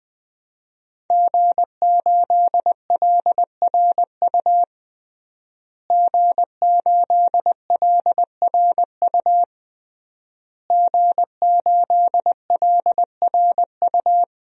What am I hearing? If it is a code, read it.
Morse code, heard as a keyed tone, about 20 wpm, 700 Hz. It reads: G8LRU G8LRU G8LRU
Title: DXCC-100 Call Signs - 5 Characters - Repeated 3x - 20wpm